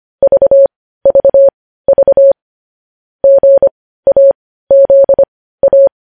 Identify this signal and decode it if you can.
Morse code, heard as a keyed tone, about 25 wpm, spaced as for 15 wpm, 550 Hz. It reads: VVV GAZA
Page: news headlines in morse code at 15 wpm